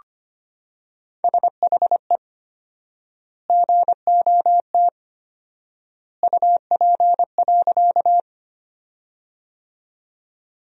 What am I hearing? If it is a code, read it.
Morse code, heard as a keyed tone, about 25 wpm, 700 Hz. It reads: SHE GOT UP.
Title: 4x Word-Spacing - Sentences from Top 300 Words - 25wpm